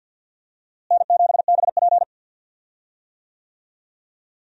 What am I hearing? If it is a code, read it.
Morse code, heard as a keyed tone, about 48 wpm, 700 Hz. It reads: N7BP
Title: USA Call Signs - 4 Characters - Repeated 2x - 50wpm